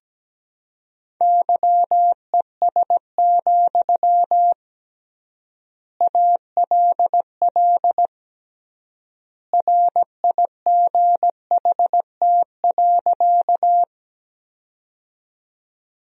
Morse code, heard as a keyed tone, about 17 wpm, 700 Hz. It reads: YES, ALL RIGHT.
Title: 3x Word-Spacing - Sentences from Top 500 Words - 17wpm